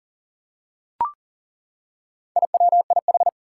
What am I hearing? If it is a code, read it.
Morse code, heard as a keyed tone, about 40 wpm, 700 Hz. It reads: IWIH
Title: Four Challenging Letter-Number Combinations 40wpm